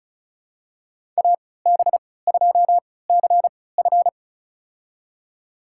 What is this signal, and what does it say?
Morse code, heard as a keyed tone, about 36 wpm, spaced as for 20 wpm, 690 Hz. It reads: AB2CF
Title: USA Call Signs - 5 Characters - 3x Character Spacing - 35wpm